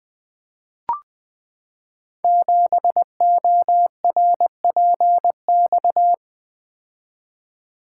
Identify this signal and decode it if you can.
Morse code, heard as a keyed tone, about 20 wpm, 700 Hz. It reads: 7ORPX
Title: Code Groups - 5 Letter:Numbers - 20wpm